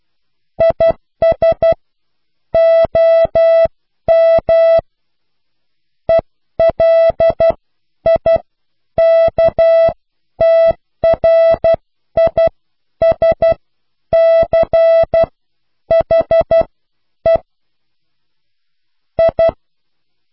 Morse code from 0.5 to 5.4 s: IS OM